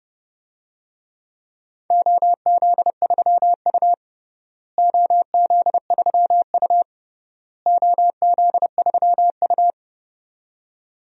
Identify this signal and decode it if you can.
Morse code, heard as a keyed tone, about 30 wpm, 700 Hz. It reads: OZ3U OZ3U OZ3U
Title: DXCC-100 Call Signs - 4 Characters - Repeated 3x - 30wpm